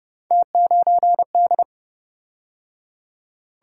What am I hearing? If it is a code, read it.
Morse code, heard as a keyed tone, about 29 wpm, 700 Hz. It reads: T9D